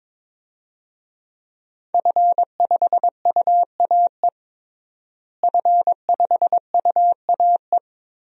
Morse code, heard as a keyed tone, about 22 wpm, 700 Hz. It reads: F5UAE F5UAE
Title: DXCC-100 Call Signs - 5 Characters - Repeated 3x - 22wpm